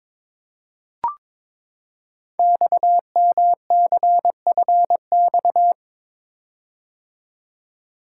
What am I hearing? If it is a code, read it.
Morse code, heard as a keyed tone, about 22 wpm, 700 Hz. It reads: XMCFX
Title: Code Groups - 5 Letters - 22wpm